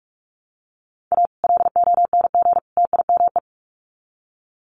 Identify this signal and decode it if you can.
Morse code, heard as a keyed tone, about 44 wpm, 700 Hz. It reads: A LONG TIME
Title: Sets of 3 Words 45wpm